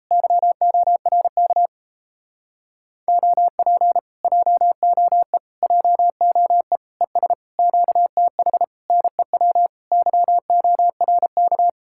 Morse code, heard as a keyed tone, about 38 wpm, 700 Hz. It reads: YORK OP JOE JOE ES QTH NEW YORK